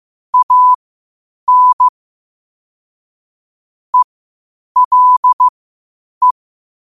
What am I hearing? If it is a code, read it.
Morse code, heard as a keyed tone, about 15 wpm, spaced as for 8 wpm, 1000 Hz. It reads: AN ELE
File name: qotdfiles / 8wpm